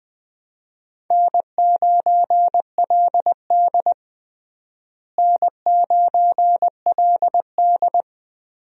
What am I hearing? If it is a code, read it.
Morse code, heard as a keyed tone, about 20 wpm, 700 Hz. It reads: N9LD N9LD
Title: USA Call Signs - 4 Characters - Repeated 3x - 20wpm